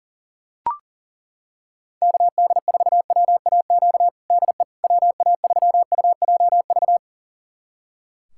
Morse code, heard as a keyed tone, about 40 wpm, 700 Hz. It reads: KD4WAQ DE WA3UJV
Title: USA Callsign DE Callsign 40wpm